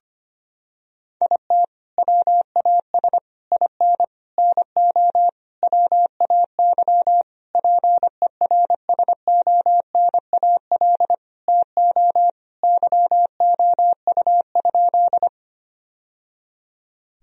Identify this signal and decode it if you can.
Morse code, heard as a keyed tone, about 25 wpm, 700 Hz. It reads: IT WAS IN NO WAY PERSONAL TO YOU?
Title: Sentences from Top 1000 Words - Review Part 4 - 25wpm